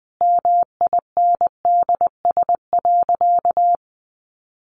Morse code, heard as a keyed tone, about 20 wpm, 700 Hz. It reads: MINDS.
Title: Easy Phrases 20wpm